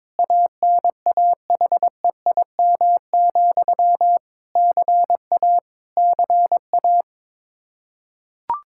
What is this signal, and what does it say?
Morse code, heard as a keyed tone, about 22 wpm, 700 Hz. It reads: ANAHEIM, CA CA
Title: K1USN SST QSO Practice 22wpm